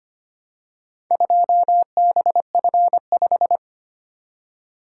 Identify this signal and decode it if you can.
Morse code, heard as a keyed tone, about 25 wpm, 690 Hz. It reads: 2BF5
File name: practice / 25wpm / Four Letter-Number Combinations - 25wpm